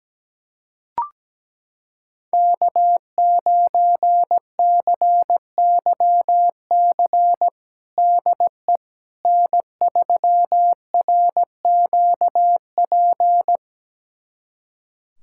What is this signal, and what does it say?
Morse code, heard as a keyed tone, about 17 wpm, 700 Hz. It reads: K9CYC DE N3RQP
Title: USA Callsign DE Callsign - 5 Characters - 17wpm